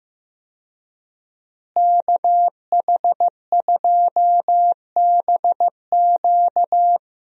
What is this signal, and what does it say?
Morse code, heard as a keyed tone, about 15 wpm, 700 Hz. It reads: KH2BQ
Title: USA Call Signs - 5 Characters - Repeated 2x - 15wpm